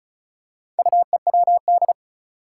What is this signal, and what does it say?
Morse code, heard as a keyed tone, about 36 wpm, 700 Hz. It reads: UEWD
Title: Four Challenging Letter-Number Combinations 35wpm